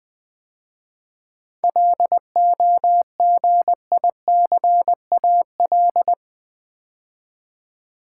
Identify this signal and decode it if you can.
Morse code, heard as a keyed tone, about 21 wpm, 690 Hz. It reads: LOGICAL
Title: 7 Letter Words - First 3 Letters Given - 20wpm